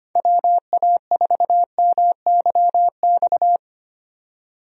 Morse code, heard as a keyed tone, about 25 wpm, 700 Hz. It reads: WA4MYX
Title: USA Call Signs - 6 Characters - 25wpm